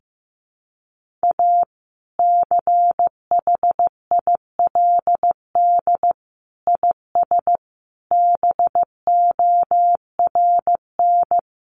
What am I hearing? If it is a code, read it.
Morse code, heard as a keyed tone, about 15 wpm, 700 Hz. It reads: A CHILD IS BORN
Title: Sentences from Top 1400 Words 15wpm